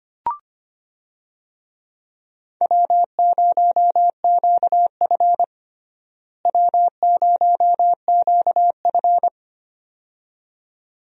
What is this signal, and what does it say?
Morse code, heard as a keyed tone, about 25 wpm, 700 Hz. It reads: W0QF W0QF
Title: USA Call Signs - 4 Characters - Repeated 2x - 25wpm